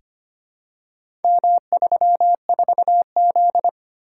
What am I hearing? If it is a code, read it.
Morse code, heard as a keyed tone, about 25 wpm, 700 Hz. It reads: M34Z
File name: practice / 25wpm / Four Letter-Number Combinations - 25wpm